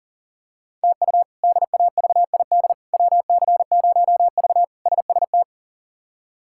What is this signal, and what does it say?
Morse code, heard as a keyed tone, about 40 wpm, 690 Hz. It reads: TU DAVID WC0V SST
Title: K1USN SST QSO Practice - 40wpm